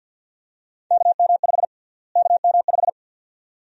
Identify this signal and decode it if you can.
Morse code, heard as a keyed tone, about 53 wpm, 700 Hz. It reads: KM5 KM5
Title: USA Call Sign Prefixes - Repeated 2x - 50wpm